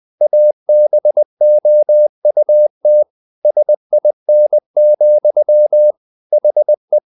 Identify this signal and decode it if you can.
Morse code, heard as a keyed tone, about 20 wpm, 600 Hz. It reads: ABOUT SIN, HE
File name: the old man and the sea part 2 20wpm